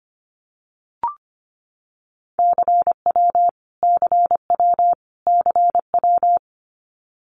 Morse code, heard as a keyed tone, about 25 wpm, 700 Hz. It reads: CW CW CW
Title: Abbreviations Repeated 3x - 25wpm